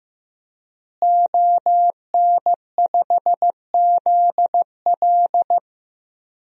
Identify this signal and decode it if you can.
Morse code, heard as a keyed tone, about 15 wpm, 700 Hz. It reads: ON5ZL
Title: DXCC-100 Call Signs - 5 Characters - Repeated 2x - 15wpm